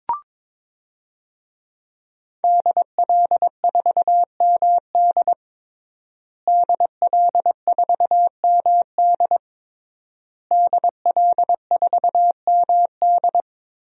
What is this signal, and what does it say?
Morse code, heard as a keyed tone, about 22 wpm, 700 Hz. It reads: DL4MD DL4MD DL4MD
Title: DXCC-100 Call Signs - 5 Characters - Repeated 3x - 22wpm